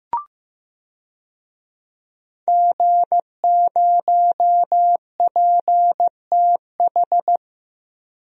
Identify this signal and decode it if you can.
Morse code, heard as a keyed tone, about 16 wpm, 700 Hz. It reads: G0PTH
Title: DXCC-100 Call Signs - 5 Characters - Repeated 2x - 15wpm